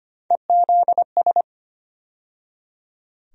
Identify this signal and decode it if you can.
Morse code, heard as a keyed tone, about 25 wpm, 700 Hz. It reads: EZS
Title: USA Call Sign Postfixes - 3 Letters 25wpm